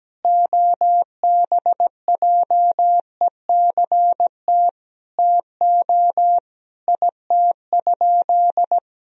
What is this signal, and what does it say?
Morse code, heard as a keyed tone, about 17 wpm, 700 Hz. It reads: OBJECT TO IT?
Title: Sentences from Top 1000 Words - Review Part 3 - 17wpm